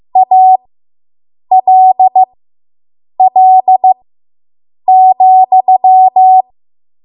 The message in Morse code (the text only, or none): ALL,